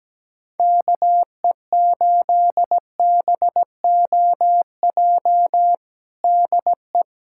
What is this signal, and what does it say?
Morse code, heard as a keyed tone, about 17 wpm, 700 Hz. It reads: KE8BOJ DE
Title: USA Callsign DE Callsign - 6 Characters - 17wpm